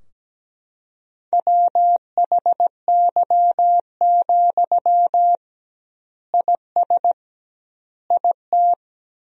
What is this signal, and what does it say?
Morse code, heard as a keyed tone, about 17 wpm, 700 Hz. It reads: WHY, IS IT
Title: 2x Word-Spacing - Sentences from Top 300 Words - 17wpm